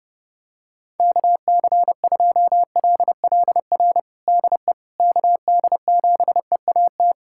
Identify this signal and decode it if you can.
Morse code, heard as a keyed tone, about 30 wpm, 700 Hz. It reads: KC2LLR DE KD7EAT
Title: USA Callsign DE Callsign - 6 Characters - 30wpm